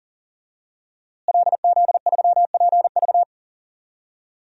Morse code, heard as a keyed tone, about 40 wpm, 700 Hz. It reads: LZ3PV
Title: DXCC-100 Call Signs - 5 Characters - Repeated 2x - 40wpm